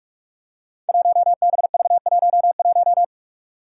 Morse code, heard as a keyed tone, about 44 wpm, 700 Hz. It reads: 1BV11
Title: Code Groups - 5 Letter:Numbers - 45wpm